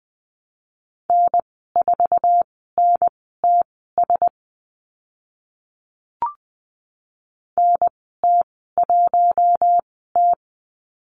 Morse code, heard as a keyed tone, about 20 wpm, spaced as for 14 wpm, 700 Hz. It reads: N4NTS NT1T
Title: 048 Lesson 9g - Number 4 - Call Signs 20wpm